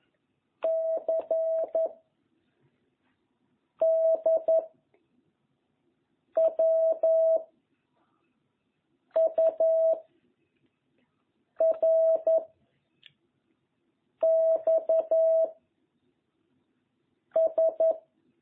Morse code, heard as a keyed tone, about 11 wpm, 660 Hz. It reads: C D W U R X S